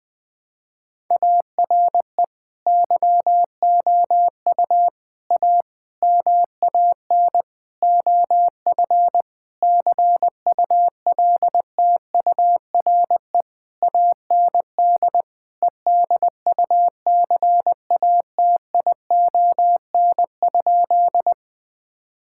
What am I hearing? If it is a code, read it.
Morse code, heard as a keyed tone, about 20 wpm, 700 Hz. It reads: ARE YOU A MAN OF CULTURE AND EDUCATION?
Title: Sentences from Top 1000 Words - Review Part 6 - 20wpm